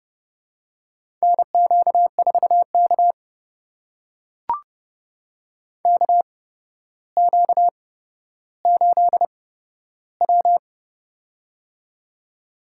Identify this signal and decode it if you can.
Morse code, heard as a keyed tone, about 29 wpm, spaced as for 13 wpm, 700 Hz. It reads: NQ4K KQ8W
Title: USA Call Signs - 4 Characters - 8x Character Spacing - 30wpm